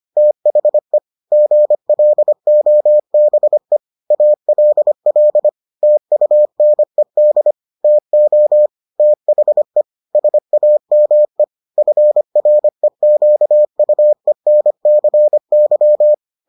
Morse code, heard as a keyed tone, about 25 wpm, 600 Hz. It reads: THE GLOBE ALL TUNED TO THE SAME FREQUENCY